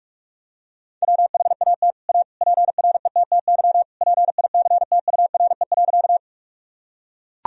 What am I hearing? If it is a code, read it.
Morse code, heard as a keyed tone, about 41 wpm, 700 Hz. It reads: WHAT A PRETTY PICTURE.
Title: Sentences from Top 1000 Words - Review Part 1 - 45wpm